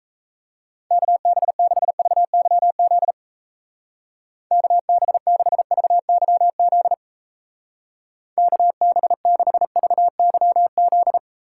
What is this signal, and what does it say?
Morse code, heard as a keyed tone, about 35 wpm, 700 Hz. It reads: KB6VYZ KB6VYZ KB6VYZ
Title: Speed Racing - USA 2x3 Call Signs 28wpm